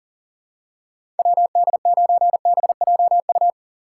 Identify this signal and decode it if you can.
Morse code, heard as a keyed tone, about 40 wpm, 700 Hz. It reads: WD9BJU
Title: USA Call Signs - 6 Characters - 2x Character Spacing - 40wpm